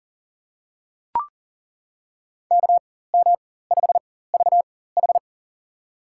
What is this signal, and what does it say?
Morse code, heard as a keyed tone, about 40 wpm, spaced as for 19 wpm, 700 Hz. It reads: KM5VH
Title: USA Call Signs - 5 Characters - 4x Character Spacing - 40wpm